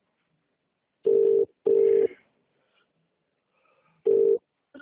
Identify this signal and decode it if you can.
Ringback tone (UK), 400 + 450 Hz.